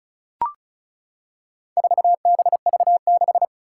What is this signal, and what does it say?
Morse code, heard as a keyed tone, about 36 wpm, 700 Hz. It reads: 4BV6
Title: Four Challenging Letter-Number Combinations 35wpm